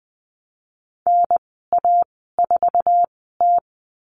Morse code, heard as a keyed tone, about 20 wpm, spaced as for 15 wpm, 700 Hz. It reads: NA4T
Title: 048 Lesson 9g - Number 4 - Call Signs 20wpm